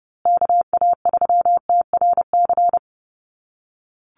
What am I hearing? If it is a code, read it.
Morse code, heard as a keyed tone, about 30 wpm, 700 Hz. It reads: KA3TRC
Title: USA Call Signs Repeated 3x 30wpm